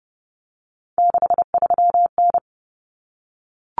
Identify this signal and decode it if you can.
Morse code, heard as a keyed tone, about 30 wpm, 700 Hz. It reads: 63N
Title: Common Cut Numbers - 3-Digit 30wpm